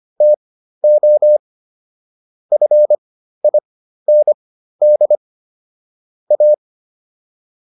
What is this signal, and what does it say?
Morse code, heard as a keyed tone, about 25 wpm, spaced as for 13 wpm, 600 Hz. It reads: TO FIND A